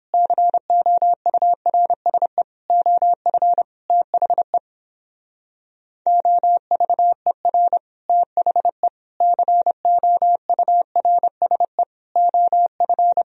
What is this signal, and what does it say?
Morse code, heard as a keyed tone, about 29 wpm, 700 Hz. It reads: COURSE OF THE OVER THE COURSE OF